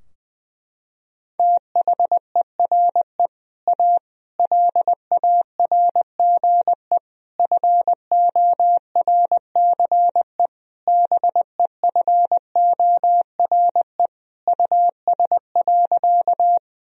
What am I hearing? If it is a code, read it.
Morse code, heard as a keyed tone, about 21 wpm, 700 Hz. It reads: THERE A LARGE FORCE BEFORE US.
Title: Sentences from Top 1000 Words - Review Part 3 - 20wpm